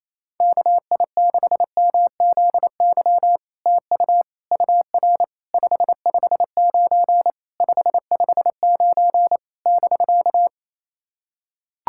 Morse code, heard as a keyed tone, about 27 wpm, 700 Hz. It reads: KI6MZY TU UR 559 559 <BK>
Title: SOTA QSO Practice 28wpm